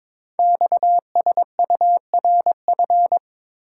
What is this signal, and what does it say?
Morse code, heard as a keyed tone, about 22 wpm, 700 Hz. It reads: XSURF
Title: Code Groups - 5 Letters - 22wpm